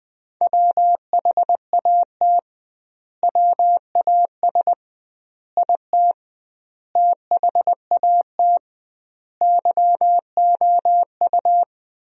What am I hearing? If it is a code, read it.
Morse code, heard as a keyed tone, about 20 wpm, 700 Hz. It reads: WHAT WAS IT THAT YOU
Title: 2x Word-Spacing - Sentences from Top 300 Words - 20wpm